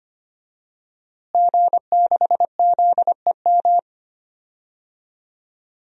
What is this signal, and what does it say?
Morse code, heard as a keyed tone, about 25 wpm, 700 Hz. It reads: G6ZEM
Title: DXCC-100 Call Signs - 5 Characters - Repeated 3x - 25wpm